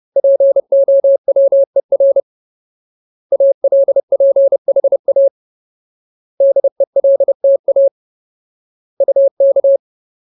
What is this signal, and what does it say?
Morse code, heard as a keyed tone, about 30 wpm, 550 Hz. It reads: POWER ALPHA DELTA UK
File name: Ultimate Word List / 30WPM Ultimate